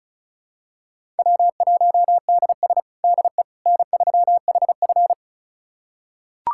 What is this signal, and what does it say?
Morse code, heard as a keyed tone, about 36 wpm, 700 Hz. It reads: W1DS DE N3HF
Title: USA Callsign DE Callsign - 4 Characters - 35wpm